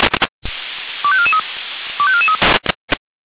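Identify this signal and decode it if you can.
Five-tone selective call (ZVEI), heard as quick runs of five tones, tones of about 70 ms. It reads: 25443 25443
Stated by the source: ZVEI five-tone selective call 25433 received on an SDR.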